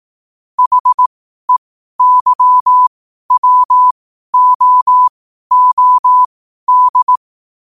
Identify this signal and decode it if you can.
Morse code, heard as a keyed tone, about 18 wpm, spaced as for 12 wpm, 1000 Hz. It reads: HEYWOOD
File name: qotdfiles / 12wpm